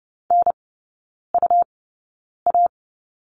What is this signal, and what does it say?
Morse code, heard as a keyed tone, about 30 wpm, spaced as for 9 wpm, 700 Hz. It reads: NUA
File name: Lesson 17e - Letter u - Three Characters 30wpm